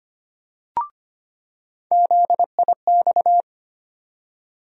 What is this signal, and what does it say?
Morse code, heard as a keyed tone, about 25 wpm, 700 Hz. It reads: ZIX
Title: USA Call Sign Postfixes - 3 Letters 25wpm